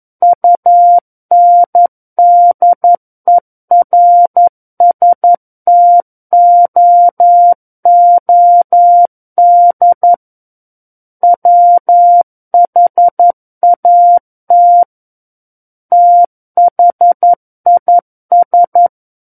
Morse code, heard as a keyed tone, about 11 wpm, 700 Hz. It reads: UNDERSTOOD WHAT THIS
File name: Book One / 02-Chapter 2 - 11 WPM